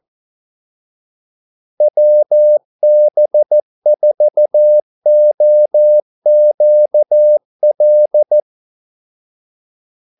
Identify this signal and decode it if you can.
Morse code, heard as a keyed tone, about 14 wpm, 600 Hz. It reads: WB4OQL
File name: Callsigns-14wpm